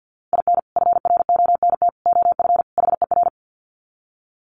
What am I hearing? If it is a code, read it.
Morse code, heard as a keyed tone, about 50 wpm, 700 Hz. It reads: IN FRONT OF HER
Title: Sets of 4 Words 50wpm